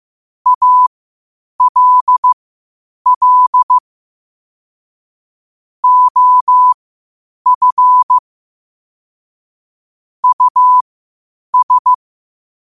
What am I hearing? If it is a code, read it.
Morse code, heard as a keyed tone, about 15 wpm, spaced as for 8 wpm, 1000 Hz. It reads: ALL OF US
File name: qotdfiles / 8wpm